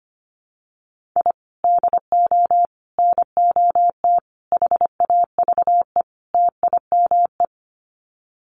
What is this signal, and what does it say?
Morse code, heard as a keyed tone, about 25 wpm, 700 Hz. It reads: I DO NOT HAVE TIME
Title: Sets of 5 Words Encore 25wpm